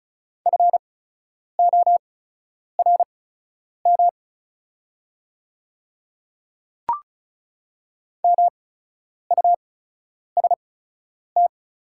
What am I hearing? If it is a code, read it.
Morse code, heard as a keyed tone, about 36 wpm, spaced as for 10 wpm, 690 Hz. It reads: FORM MUST
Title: 8x Character Spacing - Top 200 Words - 35wpm